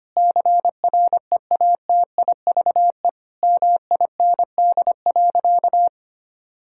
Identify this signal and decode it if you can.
Morse code, heard as a keyed tone, about 25 wpm, 700 Hz. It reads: CREATIVE MIND.
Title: Sentences from Top 1000 Words - Review Part 5 - 25wpm